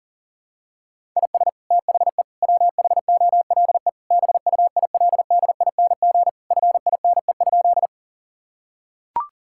Morse code, heard as a keyed tone, about 40 wpm, 700 Hz. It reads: IS THE WHOLE BUILDING FINE?
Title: Sentences from Top 1000 Words - Review Part 5 - 40wpm